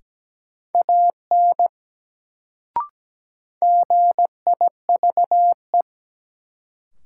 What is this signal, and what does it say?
Morse code, heard as a keyed tone, about 17 wpm, 700 Hz. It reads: AN GIVE
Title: Sets of 2 Words - New Word Drill - 17wpm